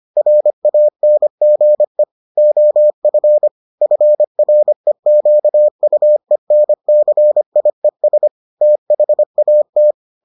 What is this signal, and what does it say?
Morse code, heard as a keyed tone, about 25 wpm, 600 Hz. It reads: RANGE OF FREQUENCIES THAT